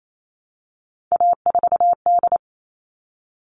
Morse code, heard as a keyed tone, about 28 wpm, 700 Hz. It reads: A4D
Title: All Cut Numbers - 3-Digit 28wpm